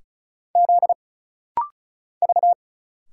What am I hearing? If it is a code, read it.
Morse code, heard as a keyed tone, about 33 wpm, 700 Hz. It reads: Z V
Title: Single Letters - Rapid-Fire 35wpm